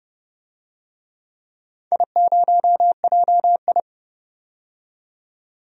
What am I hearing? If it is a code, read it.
Morse code, heard as a keyed tone, about 30 wpm, 700 Hz. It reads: I0JI
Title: DXCC-100 Call Signs - 4 Characters - 30wpm